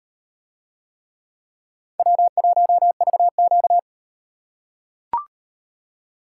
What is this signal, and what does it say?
Morse code, heard as a keyed tone, about 38 wpm, 700 Hz. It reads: W1VQ